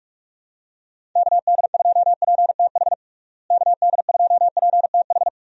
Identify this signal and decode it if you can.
Morse code, heard as a keyed tone, about 46 wpm, 700 Hz. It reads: KD2PTH KD2PTH
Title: USA Call Signs - 6 Characters - Repeated 3x - 45wpm